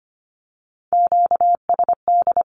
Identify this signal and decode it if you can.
Morse code, heard as a keyed tone, about 25 wpm, 700 Hz. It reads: QSD
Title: Q-Codes with Spoken Meaning - 25wpm